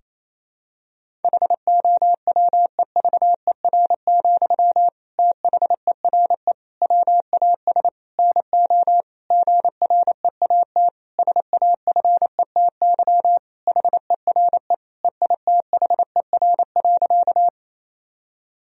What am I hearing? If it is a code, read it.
Morse code, heard as a keyed tone, about 28 wpm, 700 Hz. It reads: HOWEVER, THERE WAS NO GREAT SAFETY HERE EITHER.
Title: Sentences from Top 1000 Words - Review Part 4 - 28wpm